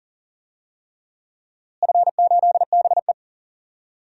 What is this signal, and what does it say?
Morse code, heard as a keyed tone, about 40 wpm, 700 Hz. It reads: F8BE
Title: DXCC-100 Call Signs - 4 Characters - Repeated 3x - 40wpm